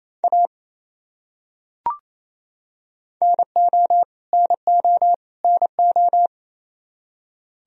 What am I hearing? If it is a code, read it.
Morse code, heard as a keyed tone, about 28 wpm, 700 Hz. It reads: A NO NO NO
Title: Top 100 Words Repeated 28wpm